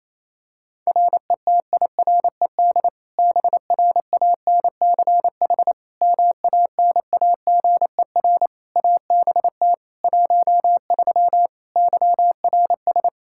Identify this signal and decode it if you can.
Morse code, heard as a keyed tone, about 28 wpm, 700 Hz. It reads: RETIRED BRANCH MANAGER ABT 13 YRS
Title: K1USN SST QSO Practice 28wpm